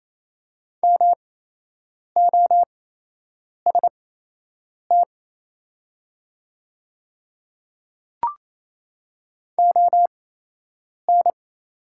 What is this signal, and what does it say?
Morse code, heard as a keyed tone, about 28 wpm, spaced as for 8 wpm, 700 Hz. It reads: MOST ON
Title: 8x Character Spacing - Top 200 Words - 28wpm